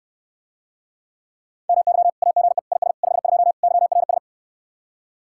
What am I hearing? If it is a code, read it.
Morse code, heard as a keyed tone, about 70 wpm, 700 Hz. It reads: MY AGE IS 52 YRS